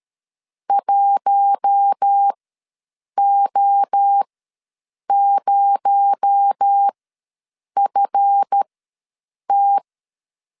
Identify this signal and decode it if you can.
Morse code, heard as a keyed tone, about 13 wpm, spaced as for 7 wpm, 800 Hz. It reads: 1O0FT
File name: LEZIONI / 7 GF9Z / 7 GF ws 12wpm 3140